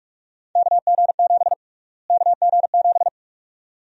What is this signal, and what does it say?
Morse code, heard as a keyed tone, about 46 wpm, 700 Hz. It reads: KG7 KG7
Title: USA Call Sign Prefixes - Repeated 2x - 45wpm